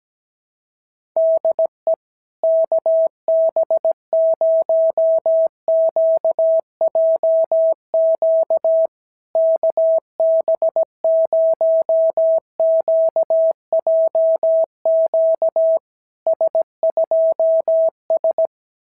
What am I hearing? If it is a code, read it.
Morse code, heard as a keyed tone, about 17 wpm, 660 Hz. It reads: DE KB0QJQ KB0QJQ S2S
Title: SOTA QSO Practice 17wpm